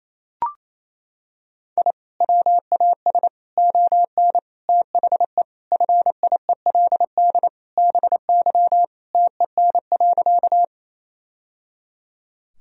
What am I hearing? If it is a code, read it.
Morse code, heard as a keyed tone, about 28 wpm, 700 Hz. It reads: I WAS ON THE FIELD BY TEN.
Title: Sentences from Top 1000 Words - Review Part 2 - 28wpm